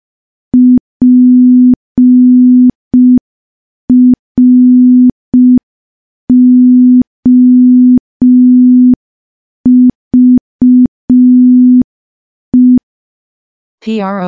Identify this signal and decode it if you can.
Morse code, heard as a keyed tone, about 5 wpm, 250 Hz. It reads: PROVE
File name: morse / morse-training-05WPM